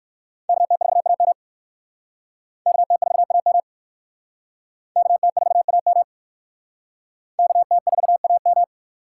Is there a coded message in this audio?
Yes: Morse code from 0.5 to 8.6 s, heard as a keyed tone, about 60 wpm, 700 Hz. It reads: KT4AM KT4AM KT4AM KT4AM